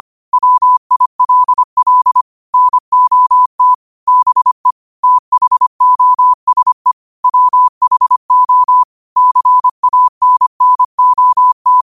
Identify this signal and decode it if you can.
Morse code, heard as a keyed tone, about 25 wpm, 1000 Hz. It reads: WILL NOT BE THOSE WHO CANNOT
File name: qotdfiles / 25wpm